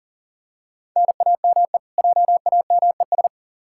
Morse code, heard as a keyed tone, about 40 wpm, 700 Hz. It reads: NAME JAMES
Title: Practice QSO - Exchange 1 - Beginning - 40wpm